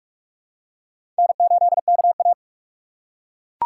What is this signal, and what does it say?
Morse code, heard as a keyed tone, about 42 wpm, 700 Hz. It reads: N8KA
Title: USA Call Signs - 4x Character Spacing - 45wpm